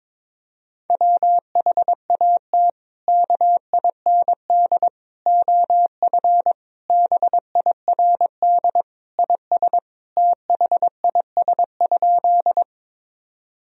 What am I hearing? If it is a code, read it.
Morse code, heard as a keyed tone, about 22 wpm, 700 Hz. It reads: WHAT KIND OF BIRD IS THIS?
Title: Sentences from Top 1000 Words - Review Part 2 - 22wpm